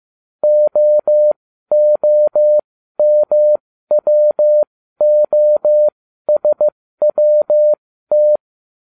Morse code, heard as a keyed tone, about 15 wpm, spaced as for 12 wpm, 600 Hz. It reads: OOMWOSWT